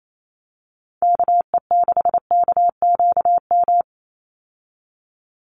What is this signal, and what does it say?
Morse code, heard as a keyed tone, about 28 wpm, 700 Hz. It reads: KE6KQM
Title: USA Call Signs 28wpm